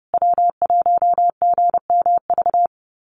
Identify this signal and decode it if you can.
Morse code, heard as a keyed tone, about 30 wpm, 700 Hz. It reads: W1GMV
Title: USA Call Signs Repeated 3x 30wpm